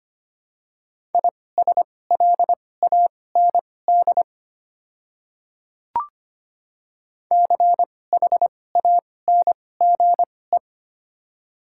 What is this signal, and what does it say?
Morse code, heard as a keyed tone, about 25 wpm, spaced as for 18 wpm, 700 Hz. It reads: ISLAND CHANGE
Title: 2x Character Spacing - Top 500 Words - 25wpm